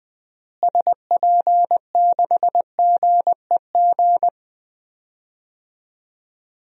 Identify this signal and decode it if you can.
Morse code, heard as a keyed tone, about 21 wpm, 700 Hz. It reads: SP6GEG
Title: DXCC-100 Call Signs - 6 Characters - Repeated 3x - 20wpm